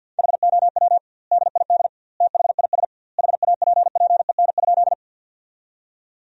Morse code, heard as a keyed tone, about 50 wpm, 700 Hz. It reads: HOW DID THIS HAPPEN?
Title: 2x Word-Spacing - Sentences from Top 400 Words - 50wpm